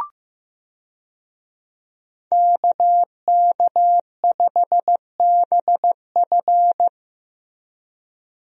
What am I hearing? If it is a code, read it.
Morse code, heard as a keyed tone, about 15 wpm, 700 Hz. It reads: KK5BF
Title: USA Call Signs - 5 Characters - 15wpm